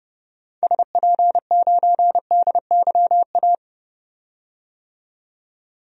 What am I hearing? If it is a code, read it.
Morse code, heard as a keyed tone, about 30 wpm, 700 Hz. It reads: SP9DYA
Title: DXCC-100 Call Signs - 6 Characters - 30wpm